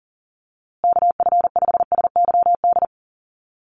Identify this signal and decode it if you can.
Morse code, heard as a keyed tone, about 40 wpm, 700 Hz. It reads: KF5SYD
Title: USA Call Signs Repeated 2x 40wpm